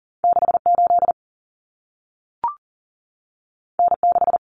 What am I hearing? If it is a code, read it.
Morse code, heard as a keyed tone, about 40 wpm, 700 Hz. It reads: 68 N6
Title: Common Cut Numbers - 2-Digit 40wpm